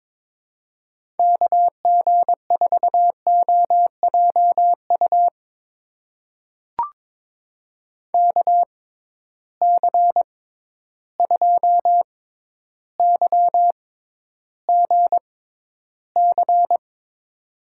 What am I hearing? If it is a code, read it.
Morse code, heard as a keyed tone, about 22 wpm, spaced as for 11 wpm, 700 Hz. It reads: KG4OJU KC2YGC
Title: USA Call Signs - 6 Characters - 6x Character Spacing - 22wpm